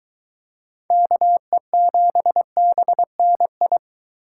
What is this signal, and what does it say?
Morse code, heard as a keyed tone, about 23 wpm, 700 Hz. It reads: KE7BNI